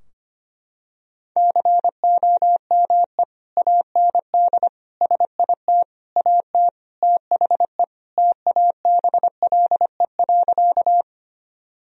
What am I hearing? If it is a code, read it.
Morse code, heard as a keyed tone, about 25 wpm, 700 Hz. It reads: COME AND SIT AT THE TABLE.
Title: Sentences from Top 1000 Words - Review Part 2 - 25wpm